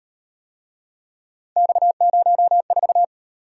Morse code, heard as a keyed tone, about 35 wpm, 700 Hz. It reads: X04